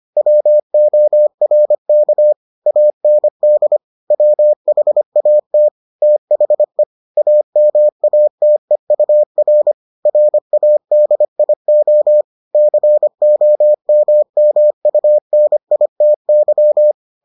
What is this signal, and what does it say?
Morse code, heard as a keyed tone, about 25 wpm, 600 Hz. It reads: WORK AND WHAT THE AMATEUR RADIO COMMUNITY